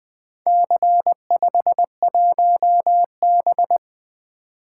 Morse code, heard as a keyed tone, about 20 wpm, 700 Hz. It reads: C51B
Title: Four Letter-Number Combinations - 20wpm